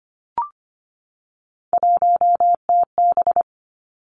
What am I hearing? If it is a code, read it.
Morse code, heard as a keyed tone, about 25 wpm, 700 Hz. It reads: 1TB
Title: Three Challenging Letter-Number Combinations 25wpm